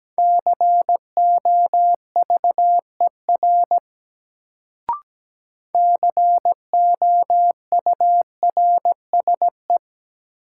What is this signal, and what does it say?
Morse code, heard as a keyed tone, about 17 wpm, 700 Hz. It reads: COVER COURSE
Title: Sentences from Top 500 Words - New Word Drill - 17wpm